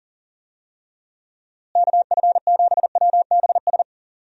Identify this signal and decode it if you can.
Morse code, heard as a keyed tone, about 37 wpm, 700 Hz. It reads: KF7WBS
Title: USA Call Signs - 6 Characters - Repeated 3x - 40wpm